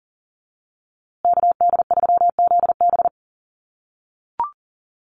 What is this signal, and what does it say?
Morse code, heard as a keyed tone, about 40 wpm, 700 Hz. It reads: KD3ZB
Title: USA Call Signs Repeated 2x 40wpm